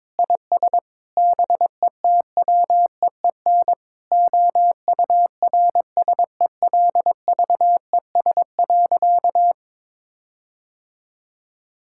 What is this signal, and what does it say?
Morse code, heard as a keyed tone, about 22 wpm, 700 Hz. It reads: IS BETWEEN OURSELVES.
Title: Sentences from Top 1600 Words 22wpm